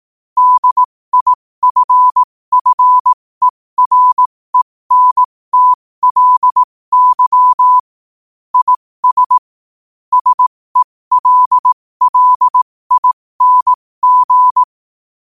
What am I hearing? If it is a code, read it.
Morse code, heard as a keyed tone, about 18 wpm, spaced as for 15 wpm, 1000 Hz. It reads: DIFFERENTLY IS SELLING
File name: qotdfiles / 15wpm